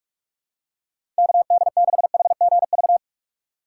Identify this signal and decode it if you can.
Morse code, heard as a keyed tone, about 45 wpm, 700 Hz. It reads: KD6HGV
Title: USA Call Signs - 6x Character Spacing - 45wpm